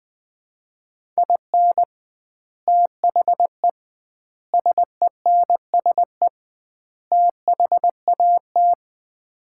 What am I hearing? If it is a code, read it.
Morse code, heard as a keyed tone, about 20 wpm, 700 Hz. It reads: IN THE SENSE THAT
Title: 2x Word-Spacing - Sets of 4 Words - 20wpm